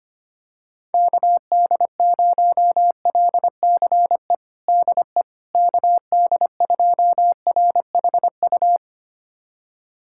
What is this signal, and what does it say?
Morse code, heard as a keyed tone, about 25 wpm, 700 Hz. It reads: KD0LCE DE KD2RHU
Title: USA Callsign DE Callsign - 6 Characters - 25wpm